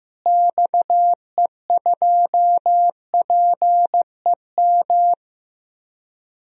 Morse code, heard as a keyed tone, about 15 wpm, 700 Hz. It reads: XE2PEM
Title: DXCC-100 Call Signs - 6 Characters - Repeated 3x - 15wpm